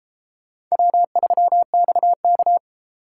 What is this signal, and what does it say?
Morse code, heard as a keyed tone, about 32 wpm, 700 Hz. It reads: W3XK